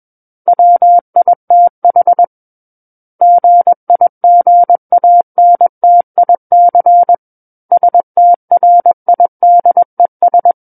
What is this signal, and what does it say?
Morse code, heard as a keyed tone, about 21 wpm, 700 Hz. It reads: WITH GIGANTIC STRIDES
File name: Book One / 12-Chapter 12 - 21 WPM